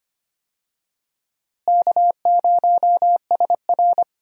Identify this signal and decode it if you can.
Morse code, heard as a keyed tone, about 25 wpm, 700 Hz. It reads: K0SR
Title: USA Call Signs - 4 Characters - Repeated 2x - 25wpm